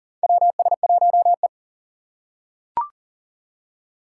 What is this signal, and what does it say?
Morse code, heard as a keyed tone, about 40 wpm, 700 Hz. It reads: WS1E
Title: USA Call Signs - 4x Character Spacing - 40wpm